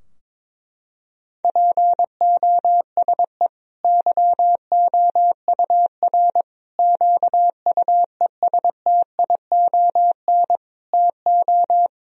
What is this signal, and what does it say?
Morse code, heard as a keyed tone, about 21 wpm, 700 Hz. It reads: POSE YOUR QUESTION TO
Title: Sentences from Top 1000 Words - Review Part 2 - 22wpm